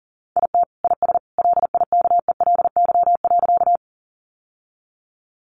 Morse code, heard as a keyed tone, about 38 wpm, 700 Hz. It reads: IT IS LIKELY.
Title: Sentences from Top 600 Words 40wpm